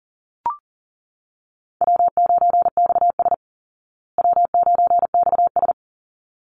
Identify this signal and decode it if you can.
Morse code, heard as a keyed tone, about 40 wpm, 700 Hz. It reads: W9XS W9XS
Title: Easy Call Signs Repeated 2x - 40wpm